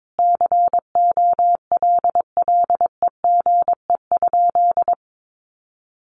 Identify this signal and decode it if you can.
Morse code, heard as a keyed tone, about 22 wpm, 700 Hz. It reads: COLLEGE?
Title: Sentences from Top 1300 Words 22wpm